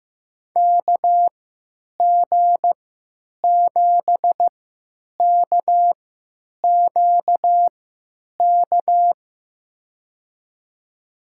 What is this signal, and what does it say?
Morse code, heard as a keyed tone, about 15 wpm, spaced as for 9 wpm, 700 Hz. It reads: KG7KQK
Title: USA Call Signs - 6 Characters - 3x Character Spacing - 15wpm